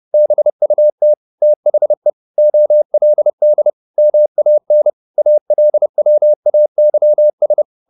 Morse code, heard as a keyed tone, about 30 wpm, 600 Hz. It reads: BUT THE OLD MAN ALWAYS
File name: the old man and the sea / the old man and the sea 1 30wpm